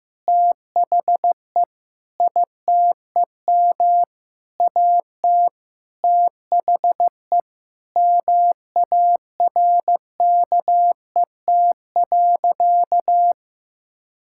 Morse code, heard as a keyed tone, about 15 wpm, 700 Hz. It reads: THE ITEM AT THE MARKET.
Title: Sentences from Top 1000 Words - Review Part 6 - 15wpm